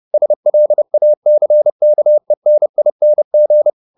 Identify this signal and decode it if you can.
Morse code, heard as a keyed tone, about 30 wpm, 600 Hz. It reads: SLACKENING